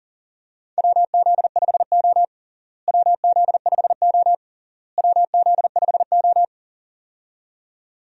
Morse code, heard as a keyed tone, about 40 wpm, 700 Hz. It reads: WZ5O WZ5O WZ5O
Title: USA Call Signs - 4 Characters - Repeated 3x - 40wpm